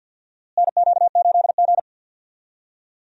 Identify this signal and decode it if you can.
Morse code, heard as a keyed tone, about 47 wpm, 700 Hz. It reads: NQ8G
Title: USA Call Signs - 4 Characters - Repeated 2x - 50wpm